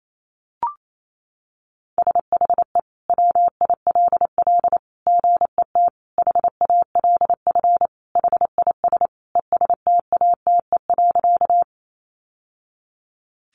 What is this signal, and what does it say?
Morse code, heard as a keyed tone, about 28 wpm, 700 Hz. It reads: SHE WILL GET HALF HIS ESTATE.
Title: Sentences from Top 1200 Words 28wpm